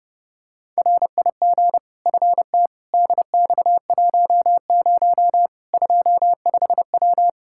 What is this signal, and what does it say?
Morse code, heard as a keyed tone, about 30 wpm, 700 Hz. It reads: RIG FT DX10 25W
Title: Practice QSO - Exchange 2 - Info - 30wpm